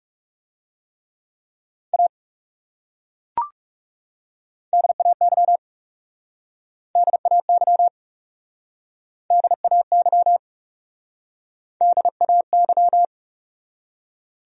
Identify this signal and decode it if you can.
Morse code, heard as a keyed tone, about 40 wpm, 700 Hz. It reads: A DAY DAY DAY DAY